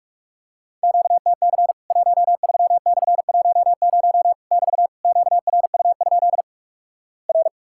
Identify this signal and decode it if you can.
Morse code, heard as a keyed tone, about 46 wpm, 700 Hz. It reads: QTC 13/10 <BT> QRU? R
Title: Worked All Europe DX Contest - QTC Exchange - 45wpm